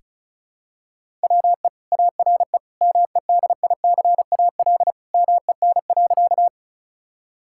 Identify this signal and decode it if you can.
Morse code, heard as a keyed tone, about 34 wpm, 700 Hz. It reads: WE ARE MEDICAL MEN.
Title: Sentences from Top 1000 Words - Review Part 5 - 35wpm